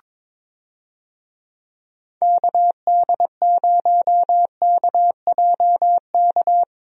Morse code, heard as a keyed tone, about 22 wpm, 700 Hz. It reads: KD0KJK
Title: USA Call Signs - 6 Characters - Repeated 3x - 22wpm